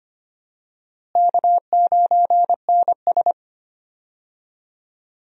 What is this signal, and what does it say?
Morse code, heard as a keyed tone, about 25 wpm, 700 Hz. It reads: K9NS
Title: USA Call Signs - 4 Characters - Repeated 3x - 25wpm